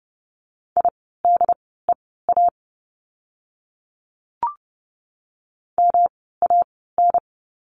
Morse code, heard as a keyed tone, about 30 wpm, spaced as for 17 wpm, 700 Hz. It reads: IDEA MAN
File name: course / 30:10wpm / 108 Lesson 18f - Letter m - Words 30wpm